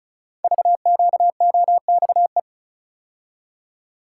Morse code, heard as a keyed tone, about 36 wpm, 700 Hz. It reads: VQOXE
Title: Code Groups - 5 Letters - 35wpm